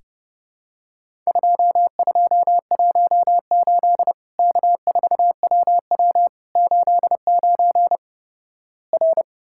Morse code, heard as a keyed tone, about 32 wpm, 700 Hz. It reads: 2218 K4WW 89 R
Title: Worked All Europe DX Contest - QTC Exchange - 30wpm